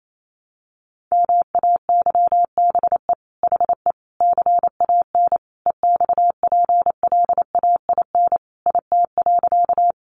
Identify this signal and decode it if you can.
Morse code, heard as a keyed tone, about 28 wpm, 700 Hz. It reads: MAYBE HE CAN EXPLAIN IT.
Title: Sentences from Top 1100 Words 28wpm